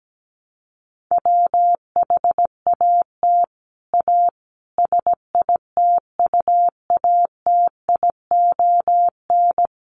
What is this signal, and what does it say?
Morse code, heard as a keyed tone, about 17 wpm, 700 Hz. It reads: WHAT A SITUATION
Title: Sentences from Top 1200 Words 17wpm